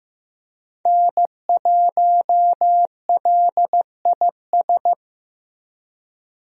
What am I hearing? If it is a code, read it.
Morse code, heard as a keyed tone, about 15 wpm, 700 Hz. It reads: N1LIS
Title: USA Call Signs - 5 Characters - Repeated 3x - 15wpm